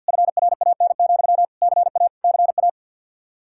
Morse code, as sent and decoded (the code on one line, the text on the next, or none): ..-. .-. .- -. --..-- / -.-. .- / -.-. .-
FRAN, CA CA